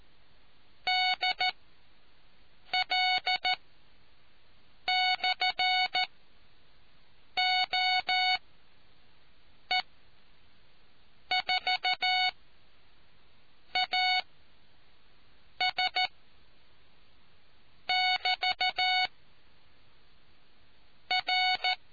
Morse code, heard as a keyed tone, about 14 wpm, spaced as for 5 wpm, 740 Hz. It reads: DL/OE4AS <BT> R